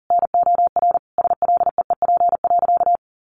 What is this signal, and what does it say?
Morse code, heard as a keyed tone, about 40 wpm, 700 Hz. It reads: NOR SLEEP.
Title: Sentences from Top 1500 Words 40wpm